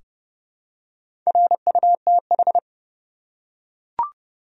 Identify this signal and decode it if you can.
Morse code, heard as a keyed tone, about 30 wpm, 700 Hz. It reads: RUTH
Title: USA Common Names - Max 4 Letters - 30wpm